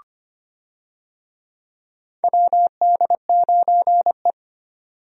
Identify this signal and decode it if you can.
Morse code, heard as a keyed tone, about 25 wpm, 700 Hz. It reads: WD9E
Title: USA Call Signs - 4 Characters - 25wpm